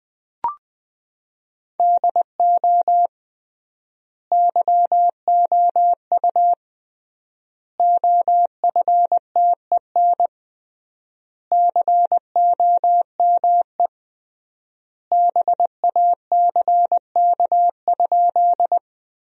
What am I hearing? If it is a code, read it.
Morse code, heard as a keyed tone, about 20 wpm, 700 Hz. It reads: DO YOU OFTEN COME BACK?
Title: 3x Word-Spacing - Sentences from Top 300 Words - 20wpm